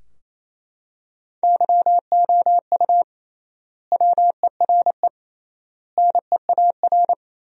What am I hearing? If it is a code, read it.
Morse code, heard as a keyed tone, about 28 wpm, 700 Hz. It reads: YOU WERE NEAR
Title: 3x Word-Spacing - Sentences from Top 200 Words - 28wpm